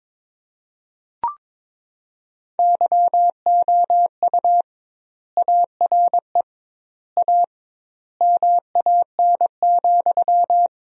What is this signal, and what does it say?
Morse code, heard as a keyed tone, about 22 wpm, 700 Hz. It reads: YOU ARE A MAN,
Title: 2x Word-Spacing - Sentences from Top 200 Words - 22wpm